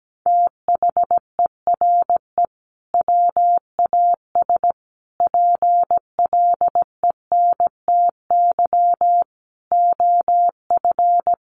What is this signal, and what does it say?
Morse code, heard as a keyed tone, about 17 wpm, 700 Hz. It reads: THERE WAS PLENTY OF
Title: Sentences from Top 1300 Words 17wpm